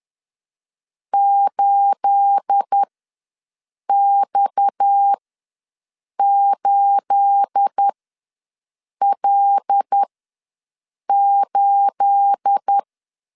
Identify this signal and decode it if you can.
Morse code, heard as a keyed tone, about 10 wpm, spaced as for 6 wpm, 800 Hz. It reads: 8X8L8
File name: LEZIONI / 9 28 ws 10wpm 4124